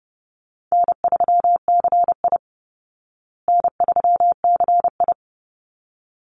Morse code, heard as a keyed tone, about 30 wpm, 700 Hz. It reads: N3CI N3CI
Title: Call Signs Repeated 3x - 30wpm